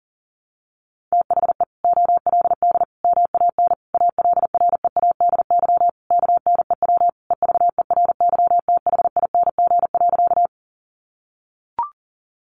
Morse code, heard as a keyed tone, about 37 wpm, 700 Hz. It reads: THE OLD MAN ALREADY KNEW EVERYTHING.
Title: Sentences from Top 700 Words 40wpm